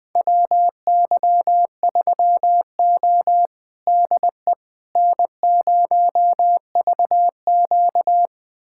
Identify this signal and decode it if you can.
Morse code, heard as a keyed tone, about 20 wpm, 700 Hz. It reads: WY3O DE N0VQ
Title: USA Callsign DE Callsign - 4 Characters - 20wpm